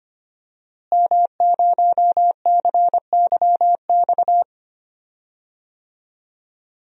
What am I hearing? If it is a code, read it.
Morse code, heard as a keyed tone, about 25 wpm, 700 Hz. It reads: M0CYX
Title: DXCC-100 Call Signs - 5 Characters - Repeated 3x - 25wpm